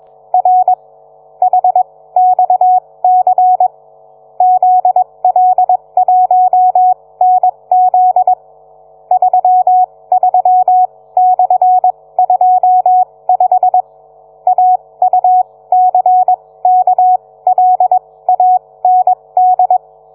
Morse code, heard as a keyed tone, about 21 wpm, spaced as for 17 wpm, 740 Hz. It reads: R HXC ZL1NZ 33/25 AUCKLAND